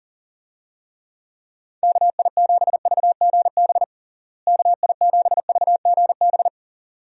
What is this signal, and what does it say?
Morse code, heard as a keyed tone, about 40 wpm, 700 Hz. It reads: KI7VGB KI7VGB
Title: USA Call Signs - 6 Characters - Repeated 2x - 40wpm